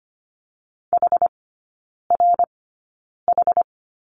Morse code, heard as a keyed tone, about 25 wpm, spaced as for 9 wpm, 690 Hz. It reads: HRH
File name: Lesson 14e - Number 2 - Three Characters 25wpm